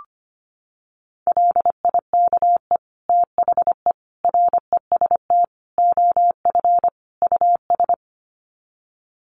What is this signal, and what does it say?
Morse code, heard as a keyed tone, about 25 wpm, 700 Hz. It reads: LIKE THE REST OF US
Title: Sets of 5 Words Encore2 25wpm